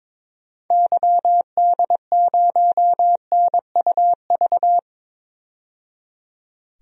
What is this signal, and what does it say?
Morse code, heard as a keyed tone, about 22 wpm, 700 Hz. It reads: YD0NUV
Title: DXCC-100 Call Signs - 6 Characters - 22wpm